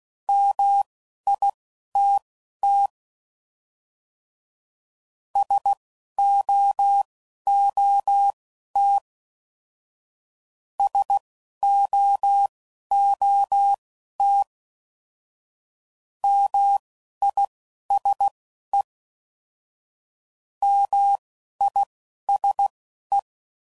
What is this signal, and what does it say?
Morse code, heard as a keyed tone, about 16 wpm, spaced as for 11 wpm, 800 Hz. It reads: MITT SOOT SOOT MISE MISE